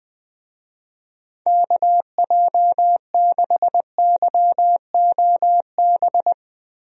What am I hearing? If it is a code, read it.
Morse code, heard as a keyed tone, about 20 wpm, 700 Hz. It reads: KJ6YOB
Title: USA Call Signs - 6 Characters - Repeated 2x - 20wpm